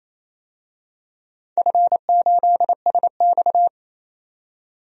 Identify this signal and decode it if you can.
Morse code, heard as a keyed tone, about 28 wpm, 700 Hz. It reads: F8SX
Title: DXCC-100 Call Signs - 4 Characters - 28wpm